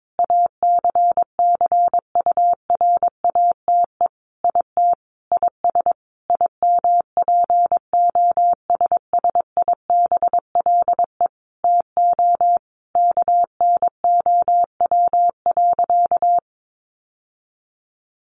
Morse code, heard as a keyed tone, about 22 wpm, 700 Hz. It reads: ACCURATE IT IS IMPOSSIBLE TO KNOW.
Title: Sentences from Top 1200 Words 22wpm